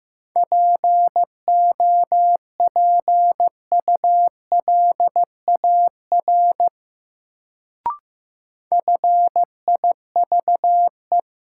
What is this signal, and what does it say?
Morse code, heard as a keyed tone, about 15 wpm, 700 Hz. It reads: POPULAR FIVE
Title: Sets of 5 Words - New Word Drill - 15wpm